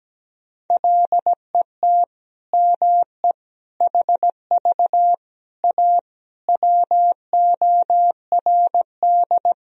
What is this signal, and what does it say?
Morse code, heard as a keyed tone, about 17 wpm, 700 Hz. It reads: LET ME HV A WORD
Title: Sentences from Top 300 Words and Abbreviations 17wpm